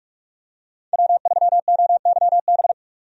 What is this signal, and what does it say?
Morse code, heard as a keyed tone, about 45 wpm, 700 Hz. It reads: W3OYB
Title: USA Call Signs - 5 Characters - 45wpm